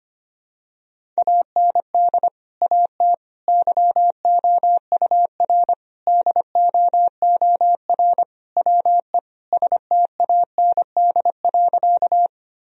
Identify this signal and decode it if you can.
Morse code, heard as a keyed tone, about 25 wpm, 700 Hz. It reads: AND AT YOUR DOOR WE STAND.
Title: Sentences from Top 1000 Words - Review Part 1 - 25wpm